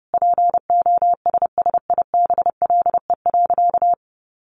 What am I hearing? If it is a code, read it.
Morse code, heard as a keyed tone, about 30 wpm, 700 Hz. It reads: POSSIBLE.
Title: Sentences from Top 1300 Words 30wpm